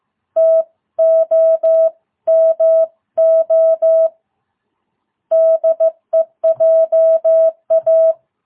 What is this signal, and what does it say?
Morse code, heard as a keyed tone, about 15 wpm, 660 Hz. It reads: TOMO DEJA